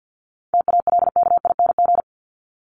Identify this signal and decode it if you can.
Morse code, heard as a keyed tone, about 50 wpm, 700 Hz. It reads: TALKING
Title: ING Suffix - Top 100 Words with Base Words 50wpm